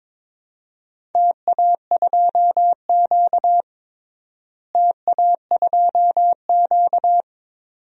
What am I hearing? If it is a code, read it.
Morse code, heard as a keyed tone, about 22 wpm, 700 Hz. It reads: TA2Q TA2Q
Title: DXCC-100 Call Signs - 4 Characters - Repeated 2x - 22wpm